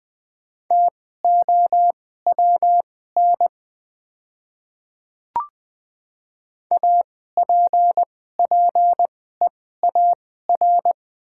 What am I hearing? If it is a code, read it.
Morse code, heard as a keyed tone, about 20 wpm, spaced as for 14 wpm, 700 Hz. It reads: TOWN APPEAR
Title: 2x Character Spacing - Top 500 Words - 20wpm